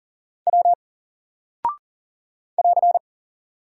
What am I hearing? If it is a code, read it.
Morse code, heard as a keyed tone, about 40 wpm, 700 Hz. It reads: W <AR>
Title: Single Character (Letter:Number:Punctuation:Prosign) Alt 40wpm